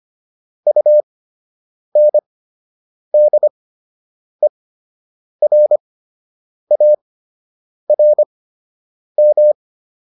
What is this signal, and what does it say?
Morse code, heard as a keyed tone, about 25 wpm, spaced as for 8 wpm, 600 Hz. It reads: UNDERARM